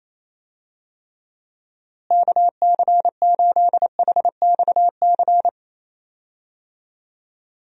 Morse code, heard as a keyed tone, about 28 wpm, 700 Hz. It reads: KC8HXC
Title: USA Call Signs - 6 Characters - Repeated 2x - 28wpm